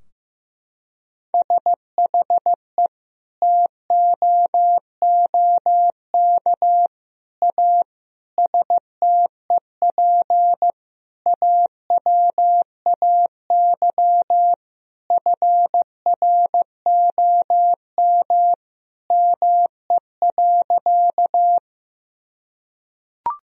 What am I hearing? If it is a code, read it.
Morse code, heard as a keyed tone, about 15 wpm, 700 Hz. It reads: SHE TOOK A STEP AWAY FROM ME.
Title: Sentences from Top 1000 Words - Review Part 2 - 15wpm